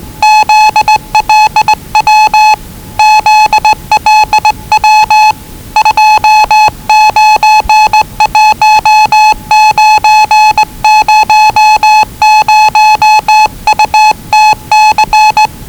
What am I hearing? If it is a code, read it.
Morse code, heard as a keyed tone, about 20 wpm, 840 Hz. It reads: ZLW ZLW 291900UTC
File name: WellingtonRadioZlwCloseDownOnHfAndMf / 01-Zlw 4277khz 417.5 qrt until 0925z